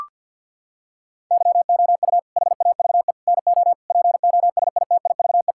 Morse code, heard as a keyed tone, about 50 wpm, 700 Hz. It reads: YOU HAVE NO POSITIVE
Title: Sentences from Top 1000 Words - Review Part 5 - 50wpm